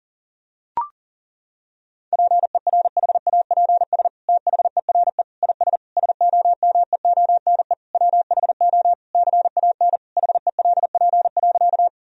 Morse code, heard as a keyed tone, about 40 wpm, 700 Hz. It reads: PERHAPS THERE IS SOMEONE WHO CAN HELP.
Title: Sentences from Top 1000 Words - Review Part 4 - 40wpm